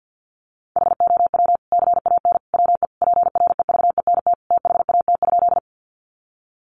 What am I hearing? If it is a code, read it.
Morse code, heard as a keyed tone, about 49 wpm, 700 Hz. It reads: HOW CAN WE PREVENT THAT?
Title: Sentences from Top 1100 Words 50wpm